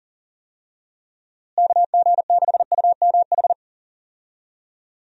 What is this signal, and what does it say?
Morse code, heard as a keyed tone, about 39 wpm, 700 Hz. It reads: KG6UMH
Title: USA Call Signs - 6 Characters - Repeated 2x - 40wpm